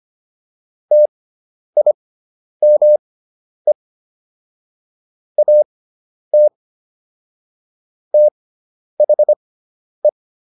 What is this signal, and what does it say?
Morse code, heard as a keyed tone, about 25 wpm, spaced as for 10 wpm, 600 Hz. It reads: TIME AT THE